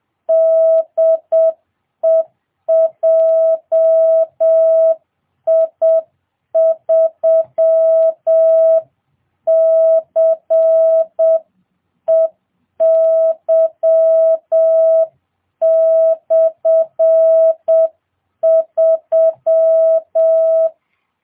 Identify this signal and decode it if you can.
Morse code, heard as a keyed tone, about 7 wpm, 650 Hz. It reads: DE JI3CEY/3